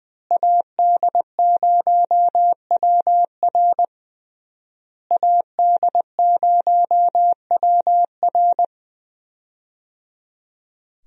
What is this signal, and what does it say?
Morse code, heard as a keyed tone, about 20 wpm, 700 Hz. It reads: AD0WR AD0WR
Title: USA Call Signs - 5 Characters - Repeated 3x - 20wpm